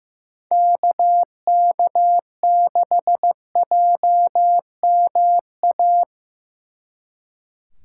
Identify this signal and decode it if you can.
Morse code, heard as a keyed tone, about 18 wpm, 700 Hz. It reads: KK6JMA
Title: Speed Racing - USA 2x3 Call Signs 15wpm